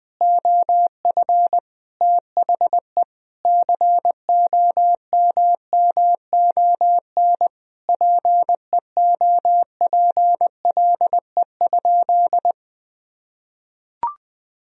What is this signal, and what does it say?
Morse code, heard as a keyed tone, about 20 wpm, 700 Hz. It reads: OF THE COMMON PEOPLE?
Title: Sentences from Top 1000 Words - Review Part 3 - 20wpm